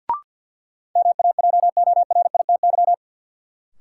Morse code, heard as a keyed tone, about 53 wpm, 700 Hz. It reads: MAJORITY
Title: Sets of 3 Words - New Word Drill - 50wpm